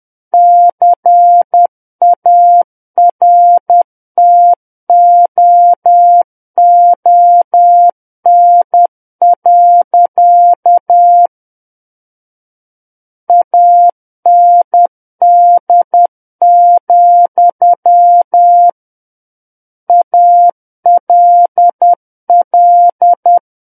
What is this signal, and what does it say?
Morse code, heard as a keyed tone, about 10 wpm, 700 Hz. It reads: CARTOON. AND, ALL
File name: Book One / 01-Chapter 1 - 10 WPM